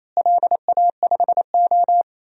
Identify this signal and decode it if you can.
Morse code, heard as a keyed tone, about 28 wpm, 700 Hz. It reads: LA5O
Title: DXCC-100 Call Signs - 4 Characters - Repeated 3x - 28wpm